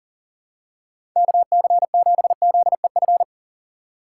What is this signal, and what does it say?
Morse code, heard as a keyed tone, about 40 wpm, 700 Hz. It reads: KC7ZEF
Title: USA Call Signs - 6 Characters - 2x Character Spacing - 40wpm